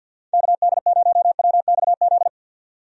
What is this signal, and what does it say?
Morse code, heard as a keyed tone, about 52 wpm, 700 Hz. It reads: KD0WXZ
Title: USA Call Signs - 6 Characters - 50wpm